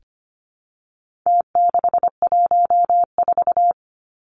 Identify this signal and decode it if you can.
Morse code, heard as a keyed tone, about 25 wpm, 700 Hz. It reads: T614
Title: Numbers - Contest Style - 4-Digit 25wpm